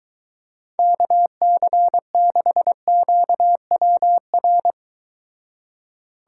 Morse code, heard as a keyed tone, about 23 wpm, 700 Hz. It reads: KC6QWR